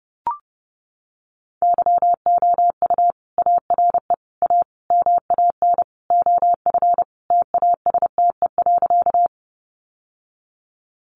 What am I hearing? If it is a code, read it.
Morse code, heard as a keyed tone, about 28 wpm, 700 Hz. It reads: YOU ARE A MAN OF TASTE.
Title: Sentences from Top 1100 Words 30wpm